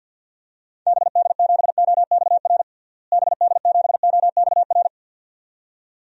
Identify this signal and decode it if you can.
Morse code, heard as a keyed tone, about 47 wpm, 700 Hz. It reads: BD7OXR BD7OXR
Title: DXCC-100 Call Signs - 6 Characters - Repeated 2x - 50wpm